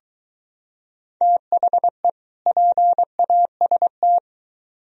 Morse code, heard as a keyed tone, about 23 wpm, 700 Hz. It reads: THE PAST